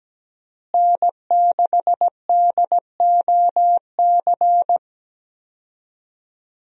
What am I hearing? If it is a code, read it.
Morse code, heard as a keyed tone, about 17 wpm, 700 Hz. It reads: N6DOC
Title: USA Call Signs - 5 Characters - Repeated 2x - 17wpm